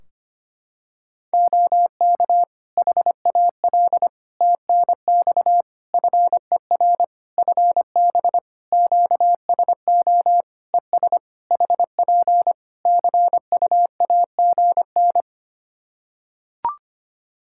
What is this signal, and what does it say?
Morse code, heard as a keyed tone, about 25 wpm, 700 Hz. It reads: OK HAL TNX FER FB QSO ES HP CUAGN
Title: Practice QSO - Exchange 4 - Ending - 25wpm